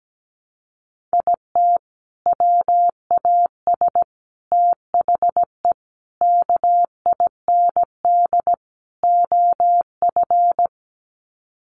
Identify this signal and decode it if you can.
Morse code, heard as a keyed tone, about 17 wpm, 700 Hz. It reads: IT WAS THE KIND OF
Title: Sets of 5 Words Encore2 17wpm